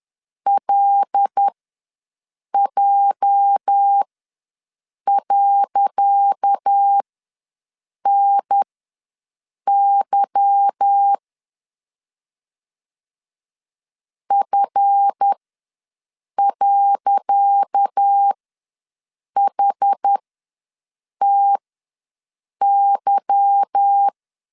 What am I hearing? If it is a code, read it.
Morse code, heard as a keyed tone, about 10 wpm, spaced as for 6 wpm, 800 Hz. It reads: LJ.NY F.HTY
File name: LEZIONI / 10 SIGNS / 10C DOT ws 10wpm 4213